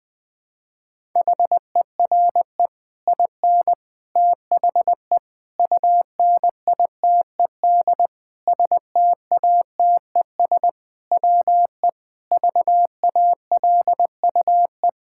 Morse code, heard as a keyed tone, about 21 wpm, 700 Hz. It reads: HERE IN THE UNITED STATES WE VALUE
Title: Sentences with 5-Word Phrases - 20wpm